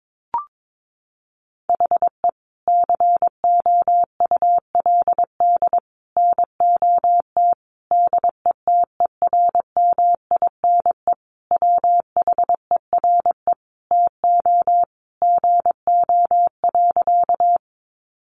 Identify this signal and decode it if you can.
Morse code, heard as a keyed tone, about 22 wpm, 700 Hz. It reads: HE COULD NOT DETERMINE WHERE TO GO.
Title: Sentences from Top 700 Words 22wpm